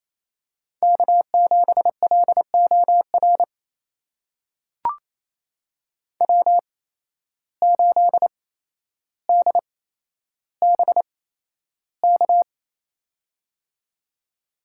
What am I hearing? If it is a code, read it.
Morse code, heard as a keyed tone, about 28 wpm, spaced as for 12 wpm, 700 Hz. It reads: K7LOR W8DBK
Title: USA Call Signs - 5 Characters - 8x Character Spacing - 28wpm